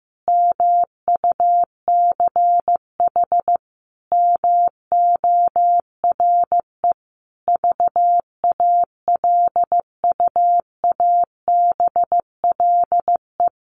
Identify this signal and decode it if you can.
Morse code, heard as a keyed tone, about 15 wpm, 700 Hz. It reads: MUCH MORE VALUABLE
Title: Sentences from Top 1100 Words 15wpm